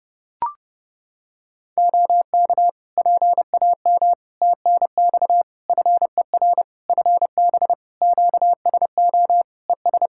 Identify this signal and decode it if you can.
Morse code, heard as a keyed tone, about 30 wpm, 700 Hz. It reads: OK PAM TNX FER FB QSO ES
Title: K1USN SST QSO Practice 30wpm